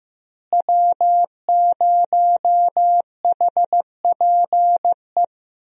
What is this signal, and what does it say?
Morse code, heard as a keyed tone, about 15 wpm, 700 Hz. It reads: W0HPE
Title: USA Call Signs - 5 Characters - Repeated 3x - 15wpm